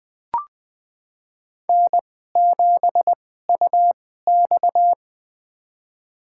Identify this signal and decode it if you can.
Morse code, heard as a keyed tone, about 20 wpm, spaced as for 15 wpm, 700 Hz. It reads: N7UX
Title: USA Call Signs - 4 Characters - 2x Character Spacing - 20wpm